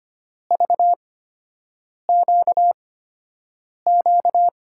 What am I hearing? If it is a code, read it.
Morse code, heard as a keyed tone, about 25 wpm, spaced as for 7 wpm, 700 Hz. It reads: VQQ
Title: USA Call Signs - 6 Characters - 8x Character Spacing - 25wpm